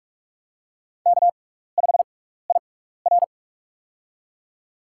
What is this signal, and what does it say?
Morse code, heard as a keyed tone, about 44 wpm, spaced as for 15 wpm, 690 Hz. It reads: K5IR
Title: USA Call Signs - 6x Character Spacing - 45wpm